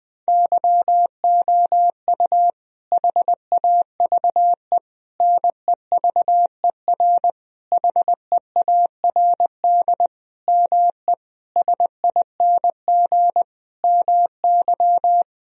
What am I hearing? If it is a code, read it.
Morse code, heard as a keyed tone, about 20 wpm, 700 Hz. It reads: YOU HAVE NEVER HEARD ME SING MY
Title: Sentences from Top 1000 Words - Review Part 2 - 20wpm